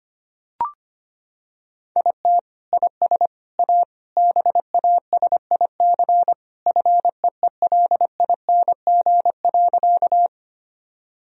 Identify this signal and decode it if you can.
Morse code, heard as a keyed tone, about 25 wpm, 700 Hz. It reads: IT IS A BASIC FEELING.
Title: Sentences from Top 1000 Words - Review Part 3 - 25wpm